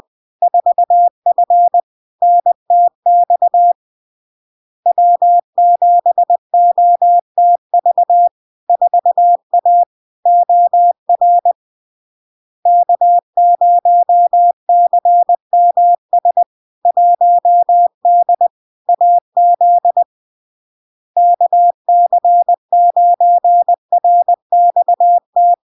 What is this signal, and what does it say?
Morse code, heard as a keyed tone, about 20 wpm, 700 Hz. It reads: 4F NTX W7OTV 4A OR K0CMS 1D AZ KC9RXT
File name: LARK-FD-2022-Log-in-CW-20wpm-700-Hz